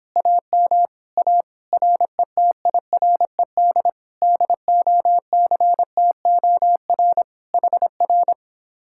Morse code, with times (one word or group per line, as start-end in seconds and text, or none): 0.2-0.9 AM
1.2-1.4 A
1.7-3.9 RETIRED
4.2-7.2 DOCTOR
7.5-8.3 HR